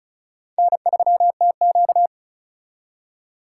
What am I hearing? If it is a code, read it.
Morse code, heard as a keyed tone, about 43 wpm, 700 Hz. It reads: N3TQ
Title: Speed Racing - USA 1x2 Call Signs 35wpm